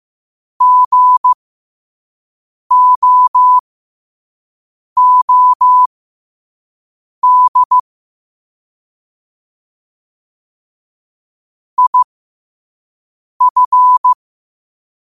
Morse code, heard as a keyed tone, about 15 wpm, spaced as for 5 wpm, 1000 Hz. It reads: GOOD IF
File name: qotdfiles / 5wpm